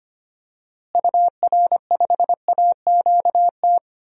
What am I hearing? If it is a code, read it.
Morse code, heard as a keyed tone, about 25 wpm, 700 Hz. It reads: UR5AQT
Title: DXCC-100 Call Signs - 6 Characters - Repeated 2x - 25wpm